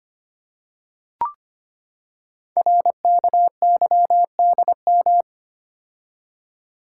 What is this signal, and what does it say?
Morse code, heard as a keyed tone, about 25 wpm, 700 Hz. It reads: RKYDM
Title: Code Groups - 5 Letters - 25wpm